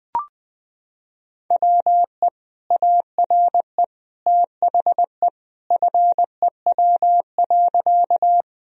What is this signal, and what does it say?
Morse code, heard as a keyed tone, about 21 wpm, 700 Hz. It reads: WE ARE THE FEW.
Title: Sentences from Top 1000 Words - Review Part 1 - 20wpm